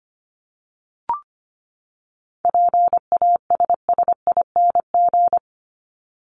Morse code, heard as a keyed tone, about 25 wpm, 700 Hz. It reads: PASSING
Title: ING Suffix - Top 100 Words with Base Words 25wpm